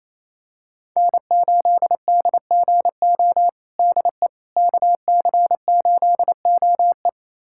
Morse code, heard as a keyed tone, about 28 wpm, 700 Hz. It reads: N8DGO DE KC8OE
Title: USA Callsign DE Callsign - 5 Characters - 28wpm